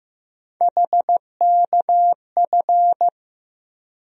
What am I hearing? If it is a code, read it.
Morse code, heard as a keyed tone, about 15 wpm, 700 Hz. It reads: HKF
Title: USA Call Sign Postfixes - 3 Letters 15wpm